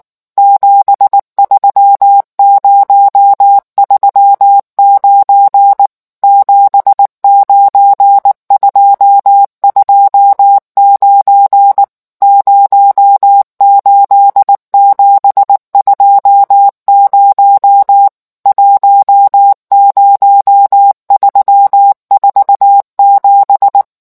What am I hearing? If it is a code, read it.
Morse code, heard as a keyed tone, about 19 wpm, 800 Hz. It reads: 73039 79229 08720 10347